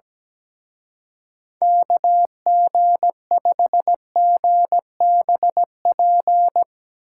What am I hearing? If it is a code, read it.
Morse code, heard as a keyed tone, about 19 wpm, 700 Hz. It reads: KG5GBP
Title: Speed Racing - USA 2x3 Call Signs 17wpm